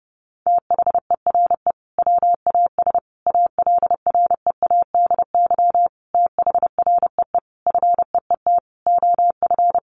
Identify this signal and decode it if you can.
Morse code, heard as a keyed tone, about 30 wpm, 700 Hz. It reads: THERE WAS ALREADY THREE FEET OF
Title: Sentences from Top 700 Words 30wpm